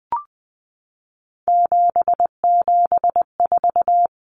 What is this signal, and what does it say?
Morse code, heard as a keyed tone, about 20 wpm, 700 Hz. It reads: 774
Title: Common Cut Numbers - 3-Digit 20wpm